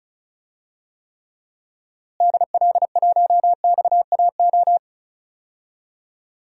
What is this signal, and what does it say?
Morse code, heard as a keyed tone, about 34 wpm, 700 Hz. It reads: DL1XAO
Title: DXCC-100 Call Signs - 6 Characters - Repeated 2x - 35wpm